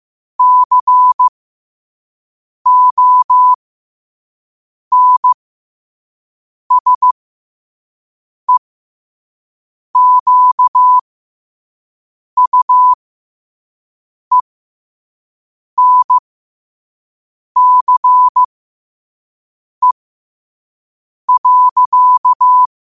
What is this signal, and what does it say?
Morse code, heard as a keyed tone, about 15 wpm, spaced as for 5 wpm, 1000 Hz. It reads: CONSEQUENCE.
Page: New quotes every day in morse code at 5 Words per minute.